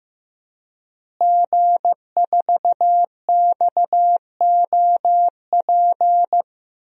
Morse code, heard as a keyed tone, about 15 wpm, 700 Hz. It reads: G4XOP
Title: DXCC-100 Call Signs - 5 Characters - Repeated 2x - 15wpm